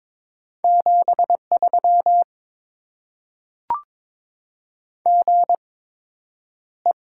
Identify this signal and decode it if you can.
Morse code, heard as a keyed tone, about 22 wpm, spaced as for 9 wpm, 700 Hz. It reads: 73 GE
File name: Top 20 QSO Elements - 8x Character Spacing - 22wpm